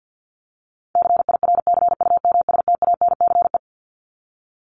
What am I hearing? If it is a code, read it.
Morse code, heard as a keyed tone, about 50 wpm, 700 Hz. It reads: CIRCUMSTANCE
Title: ES:S Suffix - Top 100 Words with Base Words 50wpm